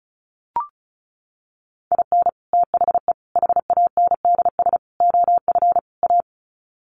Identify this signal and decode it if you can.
Morse code, heard as a keyed tone, about 34 wpm, 700 Hz. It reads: IN THE HANDS OF A
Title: Sets of 5 Words Encore 35wpm